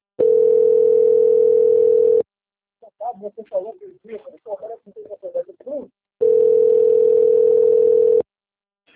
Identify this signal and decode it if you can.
Ringback tone (North America), 440 + 480 Hz, with speech.
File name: out- - - - - .7377